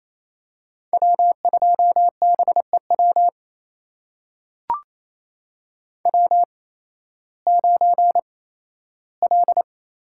Morse code, heard as a keyed tone, about 28 wpm, spaced as for 15 wpm, 700 Hz. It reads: W2BEW W9L
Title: USA Call Signs - 5 Characters - 8x Character Spacing - 28wpm